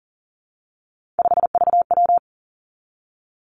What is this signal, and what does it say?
Morse code, heard as a keyed tone, about 38 wpm, 700 Hz. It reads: 5VW
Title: Three Challenging Letter-Number Combinations 40wpm